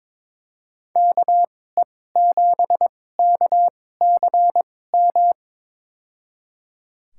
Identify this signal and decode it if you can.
Morse code, heard as a keyed tone, about 22 wpm, spaced as for 16 wpm, 700 Hz. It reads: KE7KCM
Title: USA Call Signs - 6 Characters - 2x Character Spacing - 22wpm